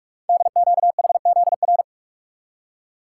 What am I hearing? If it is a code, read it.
Morse code, heard as a keyed tone, about 44 wpm, 700 Hz. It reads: DQHZR
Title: Code Groups - 5 Letter:Numbers - 45wpm